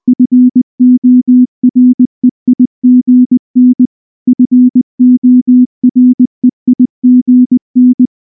Morse code, heard as a keyed tone, about 19 wpm, 250 Hz. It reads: FOREIGN FOREIGN
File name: morse / morse-training-20WPM-4